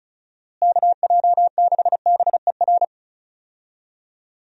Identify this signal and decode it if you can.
Morse code, heard as a keyed tone, about 34 wpm, 700 Hz. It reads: KJ6BER
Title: USA Call Signs - 6 Characters - Repeated 3x - 35wpm